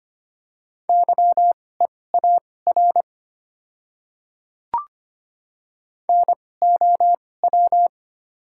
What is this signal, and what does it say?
Morse code, heard as a keyed tone, about 25 wpm, spaced as for 18 wpm, 700 Hz. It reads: YEAR NOW
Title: 2x Character Spacing - Top 200 Words - 25wpm